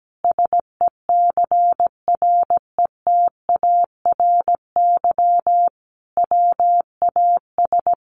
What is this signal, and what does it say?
Morse code, heard as a keyed tone, about 17 wpm, 700 Hz. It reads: SECRETARY WAS
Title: Sentences from Top 1500 Words 17wpm